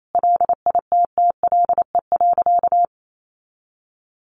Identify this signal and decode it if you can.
Morse code, heard as a keyed tone, about 28 wpm, 700 Hz. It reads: LITTLE.
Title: Sentences from Top 1500 Words 28wpm